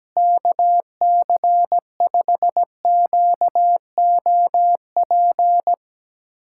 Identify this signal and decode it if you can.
Morse code, heard as a keyed tone, about 17 wpm, 700 Hz. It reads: KC5QOP
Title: USA Callsign DE Callsign 17wpm